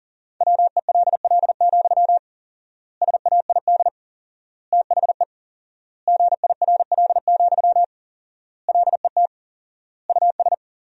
Morse code, heard as a keyed tone, about 38 wpm, 700 Hz. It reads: WELL, SAID THE GIRL, LET US
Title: 4x Word-Spacing - Sentences from Top 400 Words - 40wpm